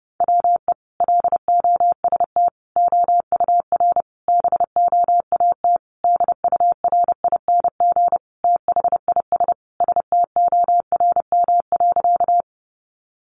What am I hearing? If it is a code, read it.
Morse code, heard as a keyed tone, about 30 wpm, 700 Hz. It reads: WE LOST OUR BOAT DURING THIS STORM.
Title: Sentences from Top 1500 Words 30wpm